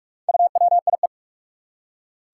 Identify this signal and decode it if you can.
Morse code, heard as a keyed tone, about 43 wpm, 700 Hz. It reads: UWIE
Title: Four Challenging Letter-Number Combinations 45wpm